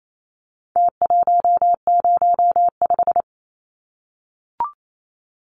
Morse code, heard as a keyed tone, about 28 wpm, 700 Hz. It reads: T105
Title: Numbers - Contest Style - 4-Digit 28wpm